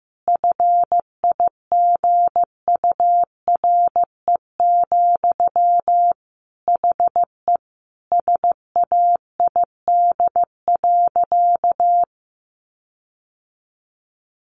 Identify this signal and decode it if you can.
Morse code, heard as a keyed tone, about 15 wpm, 700 Hz. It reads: FIGURE, HE SAID.
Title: Sentences from Top 1400 Words 15wpm